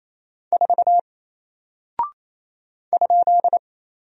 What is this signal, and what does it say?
Morse code, heard as a keyed tone, about 27 wpm, 700 Hz. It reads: 4 ?
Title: Single Character (Letter:Number:Punctuation:Prosign) 28wpm